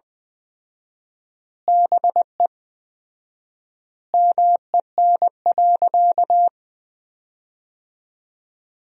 Morse code, heard as a keyed tone, about 20 wpm, 700 Hz. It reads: BE MEN.
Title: 4x Word-Spacing - Sentences from Top 300 Words - 20wpm